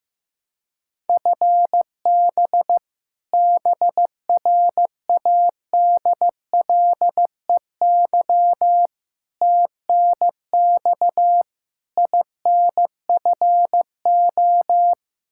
Morse code, heard as a keyed tone, about 15 wpm, 700 Hz. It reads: FB BRADLEY TNX INFO
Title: Practice QSO - Exchange 3 - More Info - 15wpm